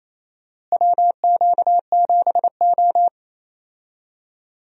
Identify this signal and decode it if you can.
Morse code, heard as a keyed tone, about 28 wpm, 700 Hz. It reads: WQ7O
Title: USA Call Signs - 4 Characters - Repeated 2x - 28wpm